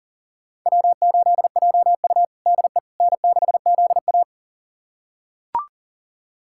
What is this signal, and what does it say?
Morse code, heard as a keyed tone, about 40 wpm, 700 Hz. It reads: W8JU DE N6ZA
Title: USA Callsign DE Callsign - 4 Characters - 40wpm